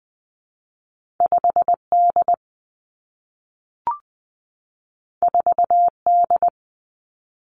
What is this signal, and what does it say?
Morse code, heard as a keyed tone, about 20 wpm, 700 Hz. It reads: 5 D 4 D
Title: Two Challenging Letter-Number Combinations 20wpm